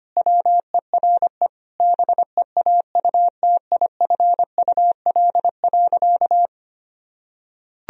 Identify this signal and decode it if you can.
Morse code, heard as a keyed tone, about 25 wpm, 700 Hz. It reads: WERE BEAUTIFUL.
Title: Sentences from Top 1000 Words - Review Part 2 - 25wpm